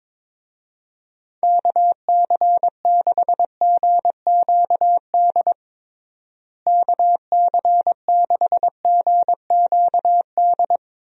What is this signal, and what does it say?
Morse code, heard as a keyed tone, about 22 wpm, 700 Hz. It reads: KC6GQD KC6GQD
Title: USA Call Signs - 6 Characters - Repeated 2x - 22wpm